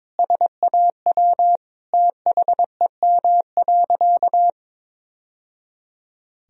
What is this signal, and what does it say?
Morse code, heard as a keyed tone, about 22 wpm, 700 Hz. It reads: SAW THEM.
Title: Sentences from Top 1000 Words - Review Part 1 - 22wpm